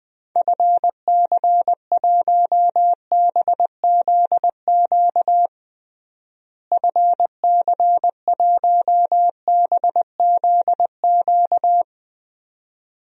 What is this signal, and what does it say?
Morse code, heard as a keyed tone, about 20 wpm, 700 Hz. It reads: FC1BZQ FC1BZQ
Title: DXCC-100 Call Signs - 6 Characters - Repeated 2x - 20wpm